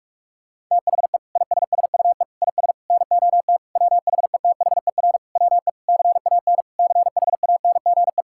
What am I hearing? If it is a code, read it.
Morse code, heard as a keyed tone, about 46 wpm, 700 Hz. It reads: THE ISSUE IS NOT WHETHER WE CAN CHANGE
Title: Sentences from Top 1000 Words - Review Part 6 - 45wpm